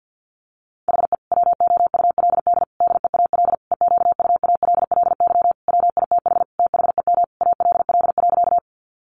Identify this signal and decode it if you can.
Morse code, heard as a keyed tone, about 50 wpm, 700 Hz. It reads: HE WOULD DEAL EQUALLY WITH THEM ALL.
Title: Sentences from Top 1400 Words 50wpm